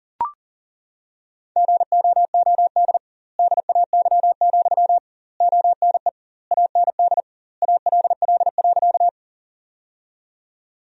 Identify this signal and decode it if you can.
Morse code, heard as a keyed tone, about 40 wpm, 700 Hz. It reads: GOOD DAY, ONE AND ALL.
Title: 2x Word-Spacing - Sentences from Top 200 Words - 40wpm